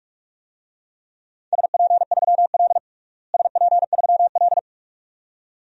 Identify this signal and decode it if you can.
Morse code, heard as a keyed tone, about 42 wpm, 700 Hz. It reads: SP3L SP3L
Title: DXCC-100 Call Signs - 4 Characters - Repeated 2x - 45wpm